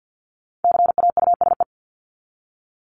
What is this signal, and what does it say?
Morse code, heard as a keyed tone, about 50 wpm, 700 Hz. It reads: CAUSE
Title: ED Suffix - Top 100 Words with Base Words 50wpm